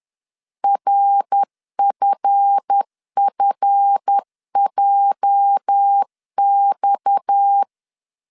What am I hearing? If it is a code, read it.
Morse code, heard as a keyed tone, about 10 wpm, 800 Hz. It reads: RFFJX
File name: CW / DET / 1 9 1500C S B 10wpm 4439